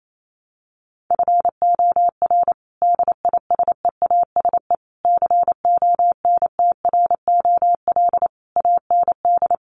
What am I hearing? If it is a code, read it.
Morse code, heard as a keyed tone, about 28 wpm, 700 Hz. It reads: FOR DISEASE CONTROL AND
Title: Sets of 5 Words Encore2 28wpm